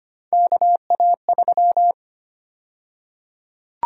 Morse code, heard as a keyed tone, about 25 wpm, 700 Hz. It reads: KA3
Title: USA Call Sign Prefixes - Repeated 2x - 25wpm